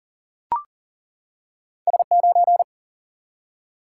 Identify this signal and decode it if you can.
Morse code, heard as a keyed tone, about 40 wpm, 700 Hz. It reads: S 9
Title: Single-Letter Single-Number 40wpm